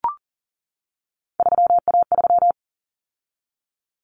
Morse code, heard as a keyed tone, about 40 wpm, 700 Hz. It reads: 3A3
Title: Common Cut Numbers - 3-Digit 40wpm